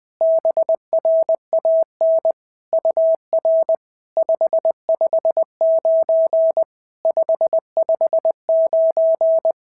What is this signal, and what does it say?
Morse code, heard as a keyed tone, about 20 wpm, 650 Hz. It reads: BRAN UR 559 559
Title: SOTA QSO Practice 20wpm